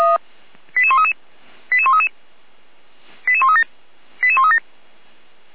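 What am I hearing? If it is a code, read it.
Five-tone selective call (ZVEI), heard as quick runs of five tones, tones of about 70 ms. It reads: 88133 88133 88137 88137
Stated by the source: ZVEI five-tone call received by SDR, from Feuerwehr Orlamünde fire brigade audio archive.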